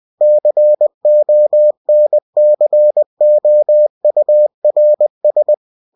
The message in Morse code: CONCOURS